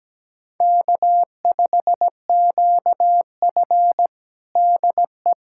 Morse code, heard as a keyed tone, about 17 wpm, 700 Hz. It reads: K5QF DE
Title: USA Callsign DE Callsign - 4 Characters - 17wpm